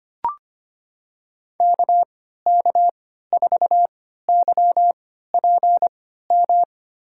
Morse code, heard as a keyed tone, about 25 wpm, spaced as for 14 wpm, 700 Hz. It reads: KK4YPM
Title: USA Call Signs - 6 Characters - 3x Character Spacing - 25wpm